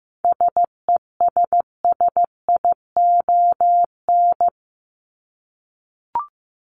Morse code, heard as a keyed tone, about 15 wpm, 700 Hz. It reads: SESSION
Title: Top 1501-1750 Words 15wpm